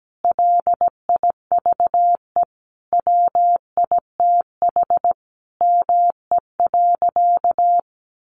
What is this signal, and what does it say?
Morse code, heard as a keyed tone, about 17 wpm, 700 Hz. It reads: LIVE WITH ME.
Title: Sentences from Top 1300 Words 17wpm